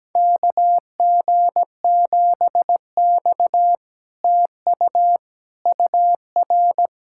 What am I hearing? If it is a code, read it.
Morse code, heard as a keyed tone, about 17 wpm, 700 Hz. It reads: KG7X TU UR
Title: SOTA QSO Practice 17wpm